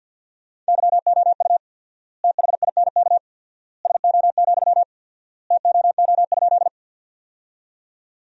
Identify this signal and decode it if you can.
Morse code, heard as a keyed tone, about 50 wpm, 700 Hz. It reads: YOU THINK SO, TOO?
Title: 4x Word-Spacing - Sentences from Top 200 Words - 50wpm